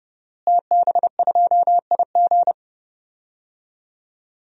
Morse code, heard as a keyed tone, about 30 wpm, 700 Hz. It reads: TB2IG
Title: DXCC-100 Call Signs - 5 Characters - Repeated 3x - 30wpm